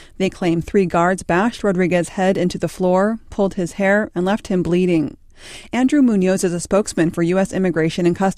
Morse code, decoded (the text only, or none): none